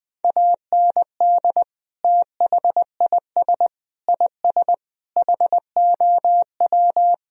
Morse code, heard as a keyed tone, about 20 wpm, 700 Hz. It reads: AND THIS IS HOW
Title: Sentences from Top 1000 Words - Review Part 2 - 20wpm